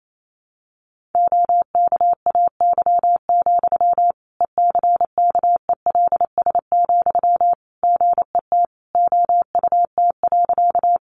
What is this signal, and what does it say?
Morse code, heard as a keyed tone, about 28 wpm, 700 Hz. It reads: OKAY, ECKELS, GET OUT.
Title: A Sound Of Thunder 28wpm